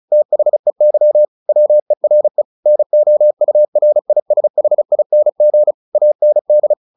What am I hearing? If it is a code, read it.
Morse code, heard as a keyed tone, about 35 wpm, 600 Hz. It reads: THEY WERE NOURISHING AND
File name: the old man and the sea 2 35wpm